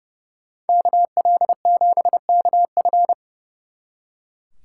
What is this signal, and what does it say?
Morse code, heard as a keyed tone, about 36 wpm, 700 Hz. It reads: KL7KF
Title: Speed Racing - USA 2x2 Call Signs 30wpm